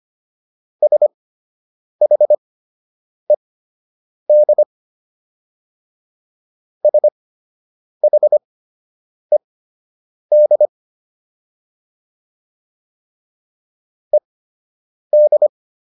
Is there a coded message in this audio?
Yes: Morse code from 0.8 to 15.5 s, heard as a keyed tone, about 25 wpm, spaced as for 8 wpm, 600 Hz. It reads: SHED SHED ED